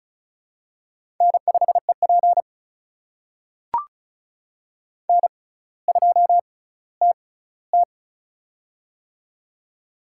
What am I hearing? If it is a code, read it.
Morse code, heard as a keyed tone, about 36 wpm, spaced as for 18 wpm, 700 Hz. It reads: N5EP N2TT
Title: USA Call Signs - 4 Characters - 6x Character Spacing - 35wpm